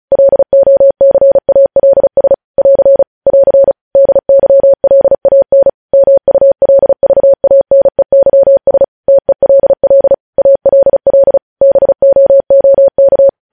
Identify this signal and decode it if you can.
Morse code, heard as a keyed tone, about 36 wpm, 550 Hz. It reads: LOCALS <AR> <AR> DYLAN MULVANEYS TELL ALL BOOK
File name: podcast / News headlines-1x-35x35